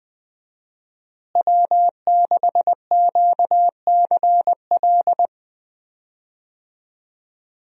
Morse code, heard as a keyed tone, about 20 wpm, 700 Hz. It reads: W6QCL
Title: USA Call Signs - 5 Characters - Repeated 3x - 20wpm